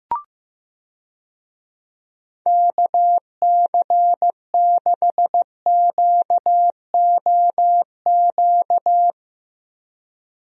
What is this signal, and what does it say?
Morse code, heard as a keyed tone, about 15 wpm, 700 Hz. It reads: KC6QOQ
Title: USA Call Signs - 6 Characters - Repeated 3x - 15wpm